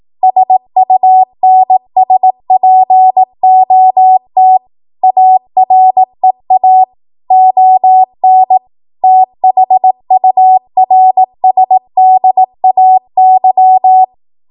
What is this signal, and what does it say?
Morse code, heard as a keyed tone, about 18 wpm, 750 Hz. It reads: SUNSPOT AREA ON THURSDAY
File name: Archive / 18 WPM / 18WPM